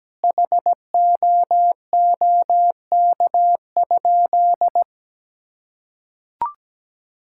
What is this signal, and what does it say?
Morse code, heard as a keyed tone, about 17 wpm, 700 Hz. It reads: HOOK?
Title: Sentences from Top 1600 Words 17wpm